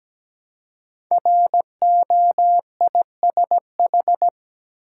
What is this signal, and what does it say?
Morse code, heard as a keyed tone, about 17 wpm, 700 Hz. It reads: ROISH
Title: Code Groups - 5 Letters - 17wpm